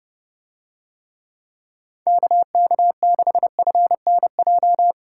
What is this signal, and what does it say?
Morse code, heard as a keyed tone, about 31 wpm, 700 Hz. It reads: KK6FNJ